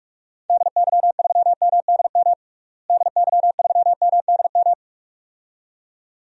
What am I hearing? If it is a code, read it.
Morse code, heard as a keyed tone, about 46 wpm, 700 Hz. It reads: DY3MDM DY3MDM
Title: DXCC-100 Call Signs - 6 Characters - Repeated 2x - 45wpm